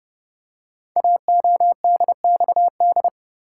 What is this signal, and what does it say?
Morse code, heard as a keyed tone, about 30 wpm, 690 Hz. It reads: AODXD
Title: Code Groups - 5 Letters - 30wpm